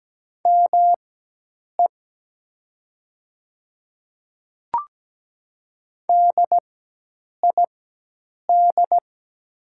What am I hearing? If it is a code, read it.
Morse code, heard as a keyed tone, about 17 wpm, spaced as for 8 wpm, 700 Hz. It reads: ME DID
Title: 4x Character Spacing - Top 200 Words - 17wpm